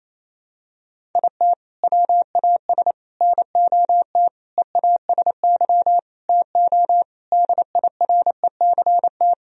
Morse code, heard as a keyed tone, about 27 wpm, 700 Hz. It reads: IT WAS NOT EASY TO DIRECT
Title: Sentences from Top 1000 Words - Review Part 2 - 28wpm